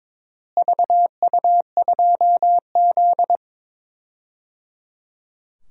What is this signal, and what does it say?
Morse code, heard as a keyed tone, about 22 wpm, 700 Hz. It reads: VU2Z
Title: DXCC-100 Call Signs - 4 Characters - Repeated 3x - 22wpm